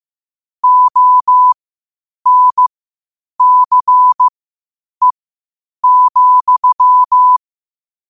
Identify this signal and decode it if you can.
Morse code, heard as a keyed tone, about 15 wpm, spaced as for 8 wpm, 1000 Hz. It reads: ONCE,
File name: qotdfiles / 8wpm